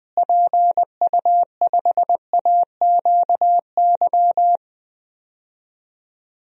Morse code, heard as a keyed tone, about 20 wpm, 700 Hz. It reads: PU5AQY
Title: DXCC-100 Call Signs - 6 Characters - Repeated 2x - 20wpm